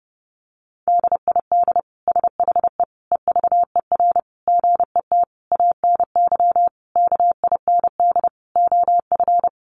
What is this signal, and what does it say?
Morse code, heard as a keyed tone, about 30 wpm, 700 Hz. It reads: DID SHE EVER GET ANY KIND OF
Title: Sentences from Top 1100 Words 30wpm